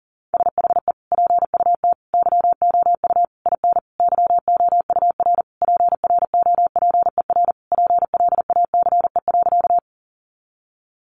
Morse code, heard as a keyed tone, about 40 wpm, 700 Hz. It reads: SHE PUT YOU IN YOUR PROPER PLACE.
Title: Sentences from Top 800 Words 40wpm